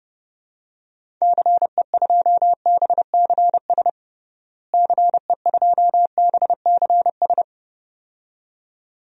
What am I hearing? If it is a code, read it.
Morse code, heard as a keyed tone, about 30 wpm, 700 Hz. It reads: CE2BCS CE2BCS
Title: DXCC-100 Call Signs - 6 Characters - Repeated 2x - 30wpm